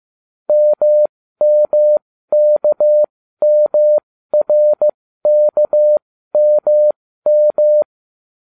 Morse code, heard as a keyed tone, about 15 wpm, spaced as for 13 wpm, 600 Hz. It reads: MMKMRKMM